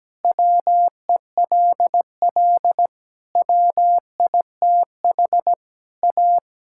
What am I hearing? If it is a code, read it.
Morse code, heard as a keyed tone, about 17 wpm, 700 Hz. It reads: WELL WITH A
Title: Sentences from Top 1600 Words 17wpm